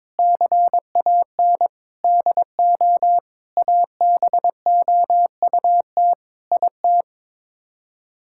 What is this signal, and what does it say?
Morse code, heard as a keyed tone, about 22 wpm, 700 Hz. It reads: CAN DO ABOUT IT
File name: practice / 22wpm / Sets of 7 Words - 22wpm